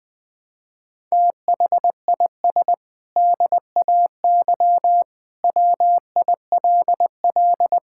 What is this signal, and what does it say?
Morse code, heard as a keyed tone, about 19 wpm, 700 Hz. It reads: THIS DAY WILL
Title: Sentences with 8-Word Phrases - 20wpm